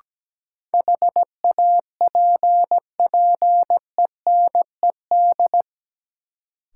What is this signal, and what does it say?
Morse code, heard as a keyed tone, about 17 wpm, 700 Hz. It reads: HAPPENED
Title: Sentences from Top 1000 Words - Review Part 2 - New Word Drill - 17wpm